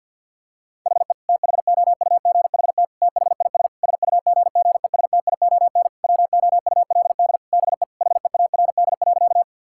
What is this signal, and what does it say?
Morse code, heard as a keyed tone, about 50 wpm, 700 Hz. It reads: HE THOUGHT THIS SUGGESTION WOULD BE HEARD.
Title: Sentences from Top 1600 Words 50wpm